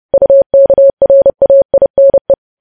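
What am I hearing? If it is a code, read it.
Morse code, heard as a keyed tone, about 30 wpm, 550 Hz. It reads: UKRAINE
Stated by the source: News headlines sent in Morse at 30 WPM.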